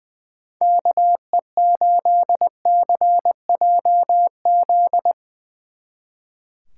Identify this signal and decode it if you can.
Morse code, heard as a keyed tone, about 20 wpm, 700 Hz. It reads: KE8CJZ
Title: Speed Racing - USA 2x3 Call Signs 20wpm